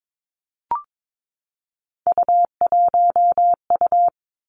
Morse code, heard as a keyed tone, about 22 wpm, 700 Hz. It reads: U1U
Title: Three Challenging Letter-Number Combinations 22wpm